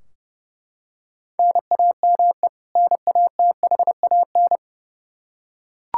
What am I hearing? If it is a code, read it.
Morse code, heard as a keyed tone, about 32 wpm, 700 Hz. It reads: NAME NATHAN
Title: Practice QSO - Exchange 1 - Beginning - 30wpm